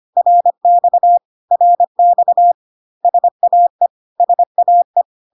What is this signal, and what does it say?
Morse code, heard as a keyed tone, about 25 wpm, 700 Hz. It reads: RX RX SAE SAE
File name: cw / CW common abbreviations alphabetical repeated 3 times 25 25wpm ik1hge